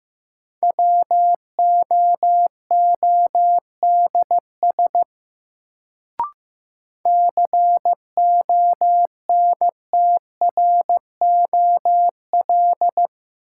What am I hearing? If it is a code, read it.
Morse code, heard as a keyed tone, about 15 wpm, 700 Hz. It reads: WOODS CONTROL
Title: Sentences from Top 1000 Words - Review Part 2 - New Word Drill - 15wpm